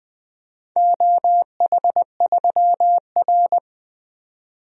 Morse code, heard as a keyed tone, about 20 wpm, 700 Hz. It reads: OH3R
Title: DXCC-100 Call Signs - 4 Characters - 20wpm